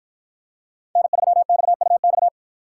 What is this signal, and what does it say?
Morse code, heard as a keyed tone, about 48 wpm, 700 Hz. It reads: N3XUX